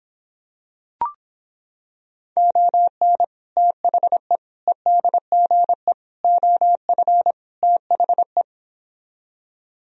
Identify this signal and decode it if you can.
Morse code, heard as a keyed tone, about 26 wpm, 700 Hz. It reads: ON THE EDGE OF THE